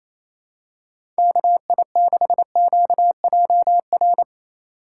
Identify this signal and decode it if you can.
Morse code, heard as a keyed tone, about 28 wpm, 700 Hz. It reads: KI6QJR
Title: USA Call Signs - 8x Character Spacing - 28wpm